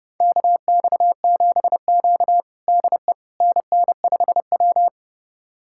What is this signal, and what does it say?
Morse code, heard as a keyed tone, about 30 wpm, 700 Hz. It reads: KX7Q DE NN5W
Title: USA Callsign DE Callsign - 4 Characters - 30wpm